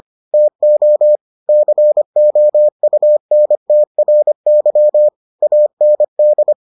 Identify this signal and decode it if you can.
Morse code, heard as a keyed tone, about 25 wpm, 600 Hz. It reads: TO COUNTRY AND